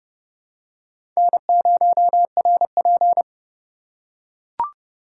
Morse code, heard as a keyed tone, about 31 wpm, 700 Hz. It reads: N0RP
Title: USA Call Signs - 4 Characters - 2x Character Spacing - 30wpm